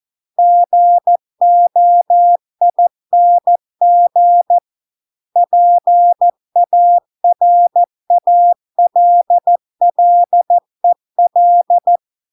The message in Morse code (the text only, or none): GOING PARALLEL